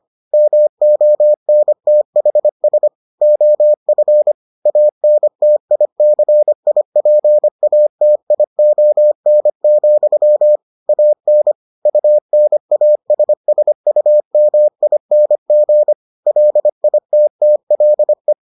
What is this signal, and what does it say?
Morse code, heard as a keyed tone, about 25 wpm, 600 Hz. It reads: MONTHS OF ANTICIPATION, AN UNASSUMING LITTLE